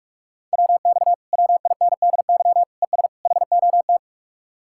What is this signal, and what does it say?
Morse code, heard as a keyed tone, about 48 wpm, 700 Hz. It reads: WX WINDY ES HOT